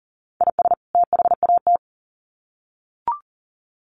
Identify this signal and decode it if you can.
Morse code, heard as a keyed tone, about 40 wpm, 700 Hz. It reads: IS THAT
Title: Sets of 2 Words 40wpm